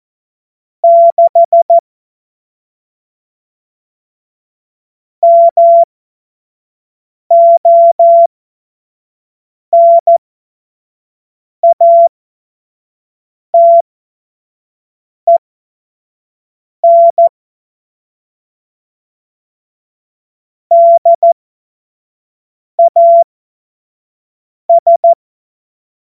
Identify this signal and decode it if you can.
Morse code, heard as a keyed tone, about 14 wpm, spaced as for 5 wpm, 680 Hz. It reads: 6 MONATEN DAS